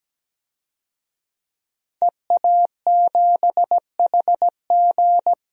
Morse code, heard as a keyed tone, about 17 wpm, 700 Hz. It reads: EA7HG
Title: DXCC-100 Call Signs - 5 Characters - 17wpm